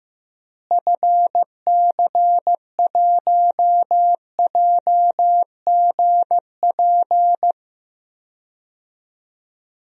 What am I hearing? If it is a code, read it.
Morse code, heard as a keyed tone, about 15 wpm, 700 Hz. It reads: FC1JGP
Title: DXCC-100 Call Signs - 6 Characters - Repeated 3x - 15wpm